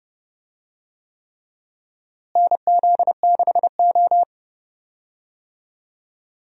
Morse code, heard as a keyed tone, about 30 wpm, 700 Hz. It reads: NZ6O
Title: USA Call Signs - 4 Characters - Repeated 3x - 30wpm